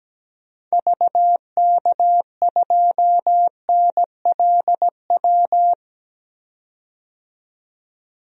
Morse code, heard as a keyed tone, about 17 wpm, 700 Hz. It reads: VK2NLW
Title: DXCC-100 Call Signs - 6 Characters - 17wpm